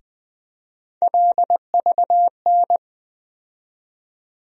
Morse code, heard as a keyed tone, about 20 wpm, 700 Hz. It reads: LVN
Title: USA Call Sign Postfixes - 3 Letters 20wpm